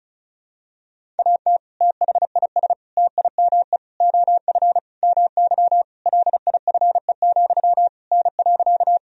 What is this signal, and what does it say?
Morse code, heard as a keyed tone, about 36 wpm, 700 Hz. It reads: AT THIS TIME OF MY LIFE, N.
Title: Sentences from Top 300 Words and Abbreviations 35wpm